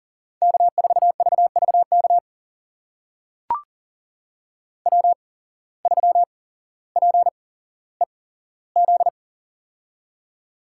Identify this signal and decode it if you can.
Morse code, heard as a keyed tone, about 37 wpm, spaced as for 17 wpm, 700 Hz. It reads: K4VVK W3PEZ
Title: USA Call Signs - 5 Characters - 8x Character Spacing - 40wpm